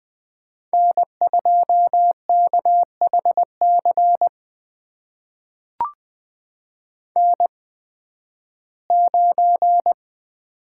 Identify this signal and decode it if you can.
Morse code, heard as a keyed tone, about 20 wpm, spaced as for 13 wpm, 700 Hz. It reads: N2KHC N9
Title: USA Call Signs - 5 Characters - 8x Character Spacing - 20wpm